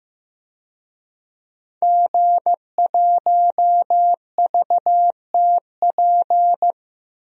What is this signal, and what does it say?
Morse code, heard as a keyed tone, about 15 wpm, 700 Hz. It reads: G1VTP
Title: DXCC-100 Call Signs - 5 Characters - 15wpm